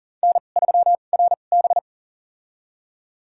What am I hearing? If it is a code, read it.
Morse code, heard as a keyed tone, about 40 wpm, spaced as for 29 wpm, 690 Hz. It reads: N3RB
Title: USA Call Signs - 4 Characters - 2x Character Spacing - 40wpm